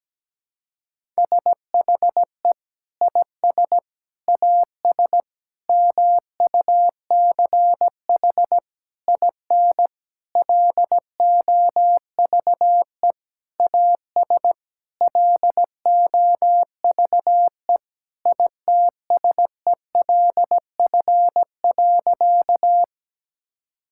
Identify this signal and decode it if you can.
Morse code, heard as a keyed tone, about 17 wpm, 700 Hz. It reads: SHE IS AS MUCH IN LOVE AS LOVE ITSELF.
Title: Sentences from Top 1000 Words - Review Part 5 - 17wpm